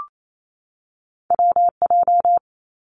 Morse code, heard as a keyed tone, about 28 wpm, 700 Hz. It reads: WJ
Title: Two Challenging Letter-Number Combinations 28wpm